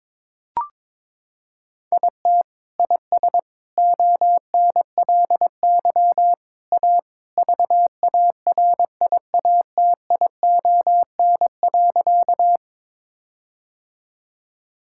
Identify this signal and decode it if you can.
Morse code, heard as a keyed tone, about 21 wpm, 700 Hz. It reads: IT IS ONLY A VARIATION.
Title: Sentences from Top 1600 Words 22wpm